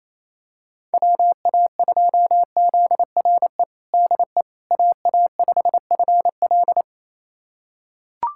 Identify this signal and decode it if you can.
Morse code, heard as a keyed tone, about 28 wpm, 700 Hz. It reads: WA2ZRE DE AA5FL
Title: USA Callsign DE Callsign 28wpm